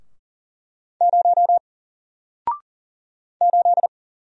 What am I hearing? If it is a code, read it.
Morse code, heard as a keyed tone, about 40 wpm, 700 Hz. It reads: 0 8
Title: Single Letter-Number 40wpm